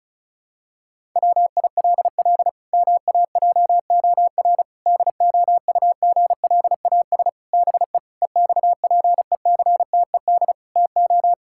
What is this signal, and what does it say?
Morse code, heard as a keyed tone, about 34 wpm, 700 Hz. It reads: WILL MAJOR DOUGLAS BE EXPECTED TO
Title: Pangram Challenge v2 - 35wpm